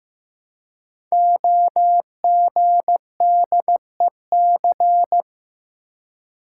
Morse code, heard as a keyed tone, about 15 wpm, 700 Hz. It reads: OGDEC
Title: Code Groups - 5 Letters - 15wpm